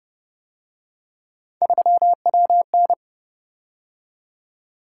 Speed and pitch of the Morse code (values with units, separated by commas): 29 wpm, 700 Hz